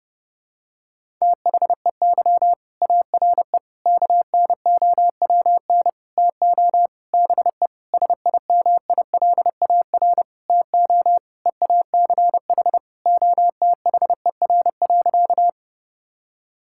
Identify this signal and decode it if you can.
Morse code, heard as a keyed tone, about 30 wpm, 700 Hz. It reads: THEY ARE KNOWN TO BE SIMILAR TO EACH OTHER.
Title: Sentences from Top 1000 Words - Review Part 5 - 30wpm